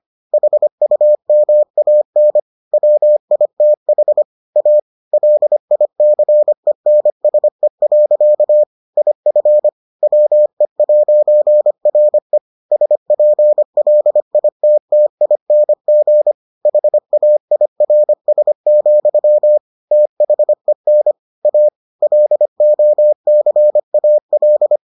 Morse code, heard as a keyed tone, about 25 wpm, 600 Hz. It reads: HUMAN WITH A LICENSE. IF WE'RE SPLITTING HAIRS, THEN A LOCAL